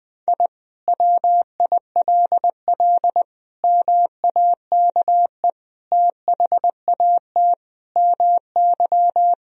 Morse code, heard as a keyed tone, about 21 wpm, 700 Hz. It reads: I WILL MAKE THAT MY
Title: Sentences from Top 1000 Words - Review Part 3 - 20wpm